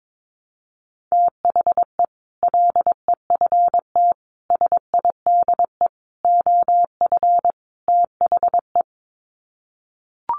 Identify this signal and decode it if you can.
Morse code, heard as a keyed tone, about 22 wpm, 700 Hz. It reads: THE LEFT SIDE OF THE
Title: Sets of 5 Words Encore 22wpm